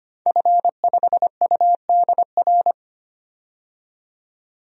Morse code, heard as a keyed tone, about 25 wpm, 700 Hz. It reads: F5UDR
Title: DXCC-100 Call Signs - 5 Characters - 25wpm